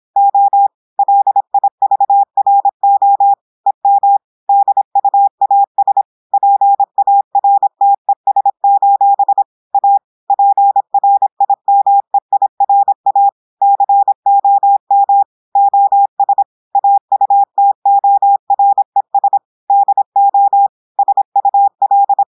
Morse code, heard as a keyed tone, about 26 wpm, 800 Hz. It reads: O LIVRO EM DUAS PARTES: A PRIMEIRA COM OS AUTORES DO SUL
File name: audios / portugais / portugais texte en clair 1 vitesse 26 mots minutes